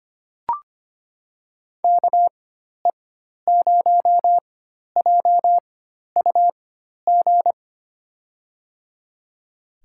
Morse code, heard as a keyed tone, about 25 wpm, spaced as for 12 wpm, 700 Hz. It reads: KE0JUG
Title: USA Call Signs - 6 Characters - 4x Character Spacing - 25wpm